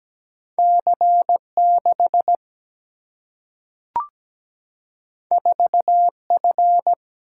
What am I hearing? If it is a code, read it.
Morse code, heard as a keyed tone, about 17 wpm, 700 Hz. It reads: C 6 4 F
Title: Single-Letter Single-Number 17wpm